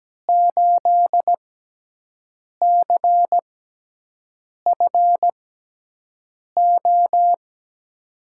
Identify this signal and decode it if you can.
Morse code, heard as a keyed tone, about 17 wpm, spaced as for 6 wpm, 700 Hz. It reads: 8CFO
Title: USA Call Signs - 5 Characters - 6x Character Spacing - 17wpm